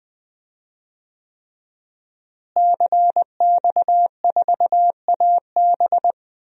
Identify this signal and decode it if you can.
Morse code, heard as a keyed tone, about 20 wpm, 700 Hz. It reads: CX4AB
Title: DXCC-100 Call Signs - 5 Characters - Repeated 3x - 20wpm